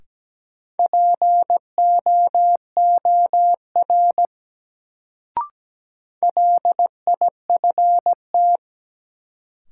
Morse code, heard as a keyed tone, about 17 wpm, 700 Hz. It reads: POOR LIFT
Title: Sentences from Top 1000 Words - New Word Drill - 17wpm